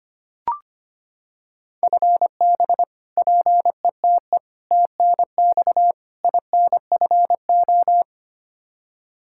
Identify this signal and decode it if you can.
Morse code, heard as a keyed tone, about 25 wpm, 700 Hz. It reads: FB PETE TNX INFO
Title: Practice QSO - Exchange 3 - More Info - 25wpm